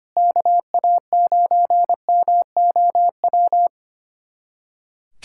Morse code, heard as a keyed tone, about 25 wpm, 700 Hz. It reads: KA9MOW
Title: USA Callsign DE Callsign - 6 Characters - 25wpm